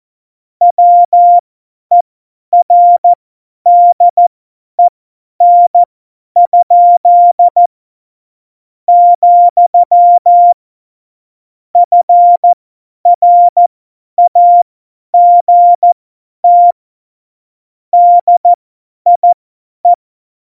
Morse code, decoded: WERDEN?, FRAGT DIE

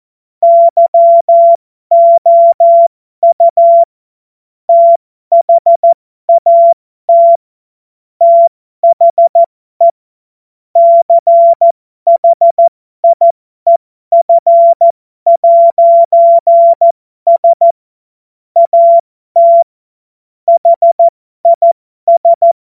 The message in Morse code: YOU THAT THE CHIEF'S AT HIS

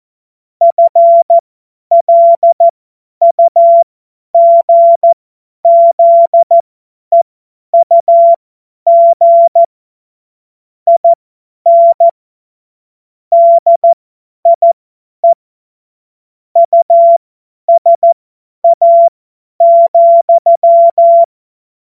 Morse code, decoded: FLUGZEUG IN DIE USA,

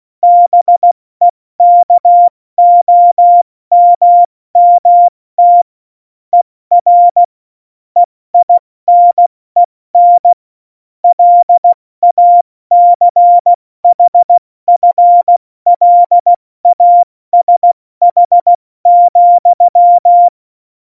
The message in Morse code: BEKOMMT ER EINEN LACHFLASH,

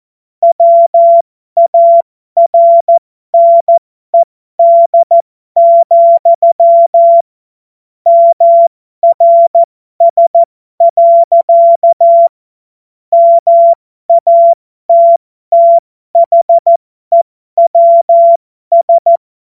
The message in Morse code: WARNED, MRS. MATTHEWS

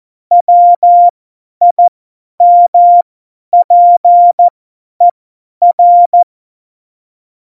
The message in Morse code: WIMPER

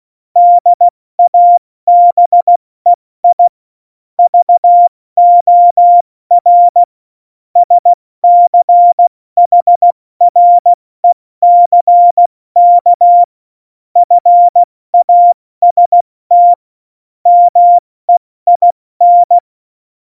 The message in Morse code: DABEI VOR SCHRECK FAST MEIN

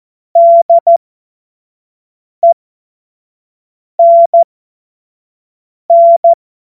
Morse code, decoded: DENN

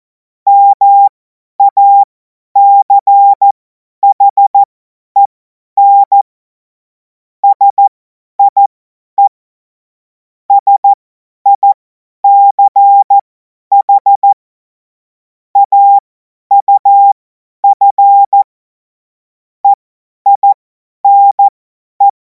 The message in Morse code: MACHEN SIE SICH AUF EINE

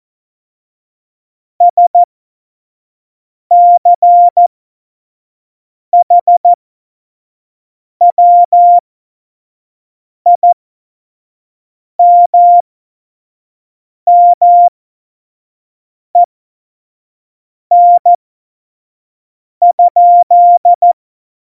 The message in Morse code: SCHWIMMEN?

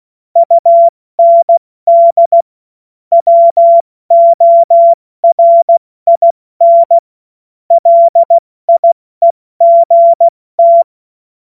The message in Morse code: UND WORIN LIEGT

